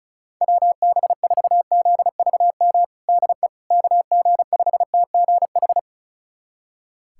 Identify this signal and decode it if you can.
Morse code, heard as a keyed tone, about 34 wpm, 700 Hz. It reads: WB4ZVM DE KG5TGH
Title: USA Callsign DE Callsign - 6 Characters - 35wpm